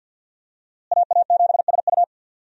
Morse code, heard as a keyed tone, about 49 wpm, 700 Hz. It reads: AA7SU
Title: USA Call Signs - 5 Characters - Repeated 3x - 50wpm